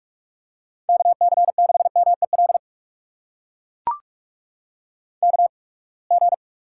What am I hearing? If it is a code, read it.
Morse code, heard as a keyed tone, about 44 wpm, spaced as for 30 wpm, 700 Hz. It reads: KC6MEL KG
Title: USA Call Signs - 6 Characters - 8x Character Spacing - 45wpm